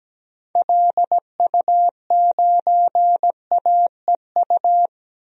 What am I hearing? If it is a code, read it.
Morse code, heard as a keyed tone, about 17 wpm, 700 Hz. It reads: LU9AEU
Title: DXCC-100 Call Signs - 6 Characters - Repeated 2x - 17wpm